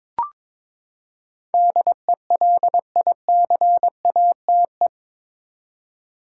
Morse code, heard as a keyed tone, about 22 wpm, 700 Hz. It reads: DELICATE
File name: practice / 22wpm / Eight Letter Words - 22wpm